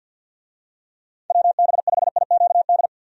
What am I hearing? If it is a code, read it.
Morse code, heard as a keyed tone, about 50 wpm, 700 Hz. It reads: WB5IQD
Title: USA Call Signs - 6 Characters - 50wpm